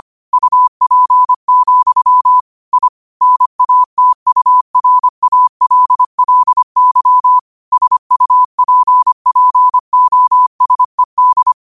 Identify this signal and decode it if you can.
Morse code, heard as a keyed tone, about 25 wpm, 1000 Hz. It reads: UP, I NATURALLY SUPPOSED
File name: 25wpm